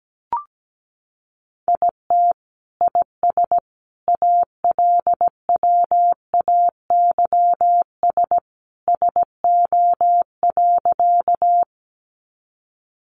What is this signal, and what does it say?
Morse code, heard as a keyed tone, about 17 wpm, 700 Hz. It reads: IT IS ALWAYS SO.
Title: Sentences from Top 300 Words 17wpm